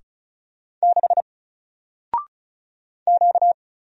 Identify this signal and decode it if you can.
Morse code, heard as a keyed tone, about 33 wpm, 700 Hz. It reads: 6 Q
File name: Single-Letter-Number-35wpm